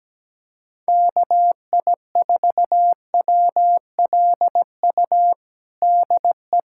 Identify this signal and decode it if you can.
Morse code, heard as a keyed tone, about 17 wpm, 700 Hz. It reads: KI4WLU DE
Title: USA Callsign DE Callsign - 6 Characters - 17wpm